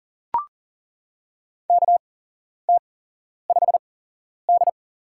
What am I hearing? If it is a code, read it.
Morse code, heard as a keyed tone, about 40 wpm, spaced as for 11 wpm, 700 Hz. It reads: KT5D
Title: USA Call Signs - 4 Characters - 8x Character Spacing - 40wpm